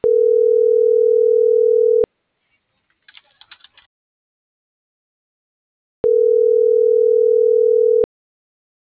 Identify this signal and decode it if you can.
Ringback tone (North America), 440 + 480 Hz.